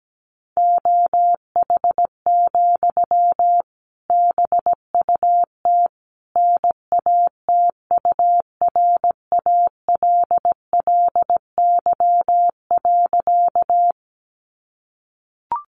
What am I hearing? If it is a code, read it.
Morse code, heard as a keyed tone, about 17 wpm, 700 Hz. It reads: OH, BUT NATURALLY.
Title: Sentences from Top 1300 Words 17wpm